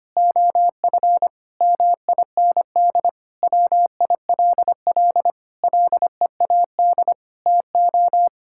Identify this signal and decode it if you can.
Morse code, heard as a keyed tone, about 25 wpm, 700 Hz. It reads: OF MIND WILL LEAD TO
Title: Sentences from Top 1000 Words - Review Part 6 - 25wpm